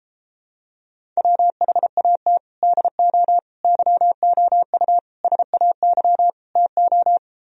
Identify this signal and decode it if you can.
Morse code, heard as a keyed tone, about 33 wpm, 700 Hz. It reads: WHAT DO YOU SAY TO